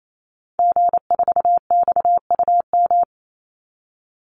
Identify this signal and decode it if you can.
Morse code, heard as a keyed tone, about 28 wpm, 700 Hz. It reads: G4XUM
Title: Call Signs - 28wpm